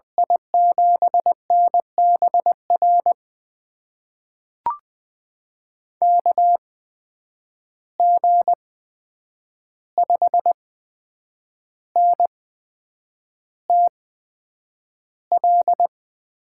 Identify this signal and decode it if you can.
Morse code, heard as a keyed tone, about 20 wpm, spaced as for 8 wpm, 700 Hz. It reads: I7NBR KG5NTL
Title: USA Call Signs - 8x Character Spacing - 20wpm